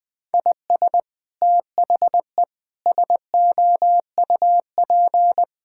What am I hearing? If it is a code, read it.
Morse code, heard as a keyed tone, about 20 wpm, 700 Hz. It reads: IS THE SOUP
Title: Sentences from Top 1000 Words - Review Part 6 - 20wpm